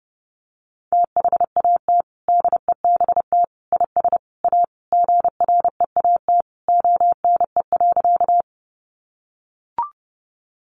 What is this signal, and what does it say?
Morse code, heard as a keyed tone, about 30 wpm, 700 Hz. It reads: THAT DEBT IS A GREAT ONE.
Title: Sentences from Top 1300 Words 30wpm